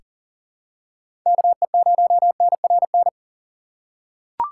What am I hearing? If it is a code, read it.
Morse code, heard as a keyed tone, about 40 wpm, 700 Hz. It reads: KE0NRN
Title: USA Call Signs - 6 Characters - 2x Character Spacing - 40wpm